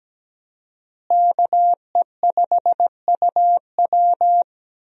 Morse code, heard as a keyed tone, about 17 wpm, 700 Hz. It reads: KE5UW
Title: USA Call Signs - 5 Characters - Repeated 2x - 17wpm